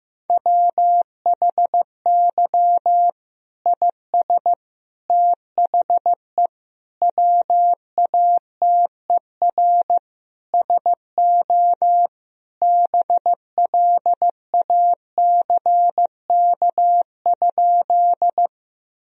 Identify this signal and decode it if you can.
Morse code, heard as a keyed tone, about 15 wpm, 700 Hz. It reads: WHY IS THE WATER SO BLACK?
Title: Sentences from Top 1000 Words - Review Part 2 - 15wpm